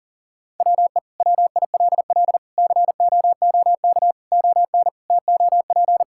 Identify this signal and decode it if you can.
Morse code, heard as a keyed tone, about 37 wpm, 700 Hz. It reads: WE WILL COOK ON TOP
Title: Sentences from Top 1000 Words - Review Part 6 - 40wpm